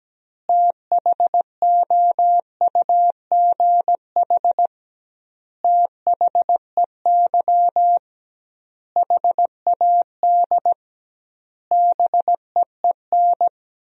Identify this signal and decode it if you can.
Morse code, heard as a keyed tone, about 17 wpm, 700 Hz. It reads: THOUGH THEY HAD BEEN
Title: 2x Word-Spacing - Sentences from Top 400 Words - 17wpm